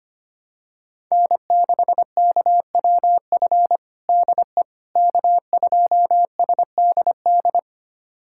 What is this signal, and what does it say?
Morse code, heard as a keyed tone, about 25 wpm, 700 Hz. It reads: N6KWF DE K2SDD
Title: USA Callsign DE Callsign - 5 Characters - 25wpm